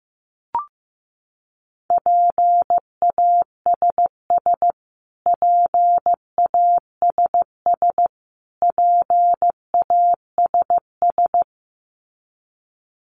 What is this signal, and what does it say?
Morse code, heard as a keyed tone, about 15 wpm, 700 Hz. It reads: PASS PASS PASS
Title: Top 500 Words Repeated 15wpm